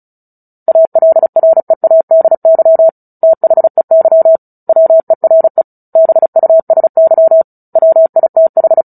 Morse code, heard as a keyed tone, about 34 wpm, 650 Hz. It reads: ALREADY THEY WERE BUSY WITH
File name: Book Two / 09-Chapter 9 - 35 WPM